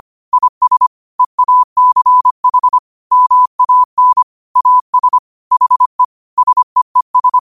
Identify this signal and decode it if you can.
Morse code, heard as a keyed tone, about 25 wpm, 1000 Hz. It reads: IS EACH MAN AS HE SEES